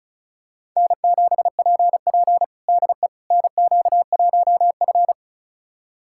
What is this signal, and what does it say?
Morse code, heard as a keyed tone, about 36 wpm, 700 Hz. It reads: N7PP DE NQ1F
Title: USA Callsign DE Callsign - 4 Characters - 35wpm